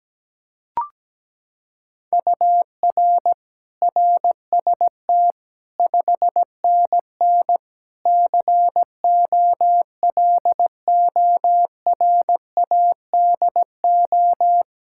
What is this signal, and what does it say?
Morse code, heard as a keyed tone, about 17 wpm, 700 Hz. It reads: UR RST 5NN COLORADO
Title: Practice QSO - Exchange 1 - Beginning - 17wpm